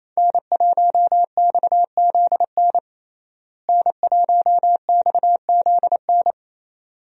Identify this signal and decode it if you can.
Morse code, heard as a keyed tone, about 27 wpm, 700 Hz. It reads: N1XZN N1XZN
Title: USA Call Signs - 5 Characters - Repeated 2x - 28wpm